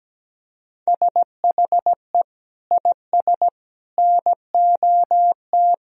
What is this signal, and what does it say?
Morse code, heard as a keyed tone, about 17 wpm, 700 Hz. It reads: SHE IS NOT
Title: Sentences from Top 1000 Words - Review Part 2 - 17wpm